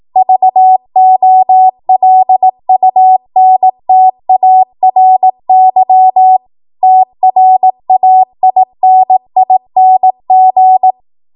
Morse code, heard as a keyed tone, about 18 wpm, 750 Hz. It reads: VOLUNTARY TRAINING
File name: Morse / 18WPM